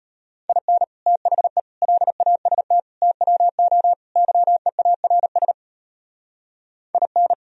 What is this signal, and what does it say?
Morse code, heard as a keyed tone, about 33 wpm, 700 Hz. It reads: IN THE LAST TWO YEARS IN